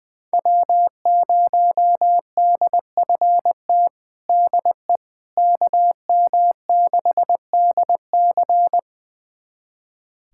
Morse code, heard as a keyed tone, about 21 wpm, 700 Hz. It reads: W0DFT DE KM6DC
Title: USA Callsign DE Callsign - 5 Characters - 20wpm